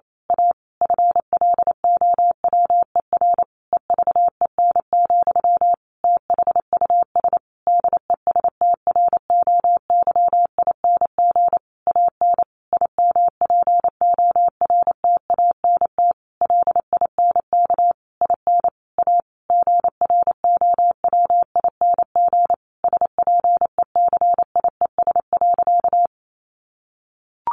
Morse code, heard as a keyed tone, about 28 wpm, 700 Hz. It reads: A FLOWER EVEN, THUS DESTROYING AN IMPORTANT LINK IN A GROWING SPECIES.
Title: A Sound Of Thunder 28wpm